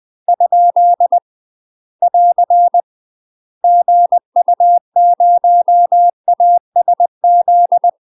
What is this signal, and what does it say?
Morse code, heard as a keyed tone, about 20 wpm, 700 Hz. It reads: ? <AR> GU0ASZ
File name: UK MASTER QSOs TRAINING / 20WPM / UK MASTER QSOs 0007